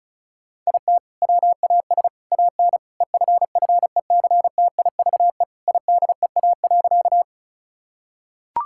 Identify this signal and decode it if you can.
Morse code, heard as a keyed tone, about 36 wpm, 700 Hz. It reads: IT WAS AN EFFECTIVE IDEA.
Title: Sentences from Top 1000 Words - Review Part 6 - 35wpm